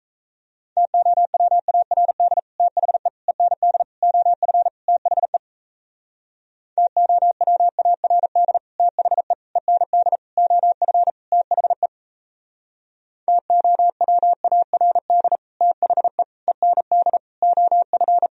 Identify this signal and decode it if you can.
Morse code, heard as a keyed tone, about 35 wpm, 700 Hz. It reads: TOWARD THE END OF THE TOWARD THE END OF THE TOWARD THE END OF